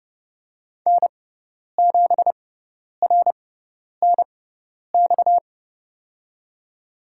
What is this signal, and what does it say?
Morse code, heard as a keyed tone, about 30 wpm, spaced as for 10 wpm, 700 Hz. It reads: N7RNX
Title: USA Call Signs - 6x Character Spacing - 30wpm